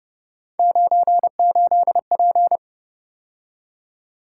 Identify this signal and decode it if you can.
Morse code, heard as a keyed tone, about 29 wpm, 700 Hz. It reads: 98P